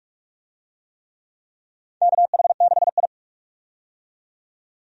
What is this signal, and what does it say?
Morse code, heard as a keyed tone, about 44 wpm, 700 Hz. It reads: KH6I
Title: DXCC-100 Call Signs - 4 Characters - 45wpm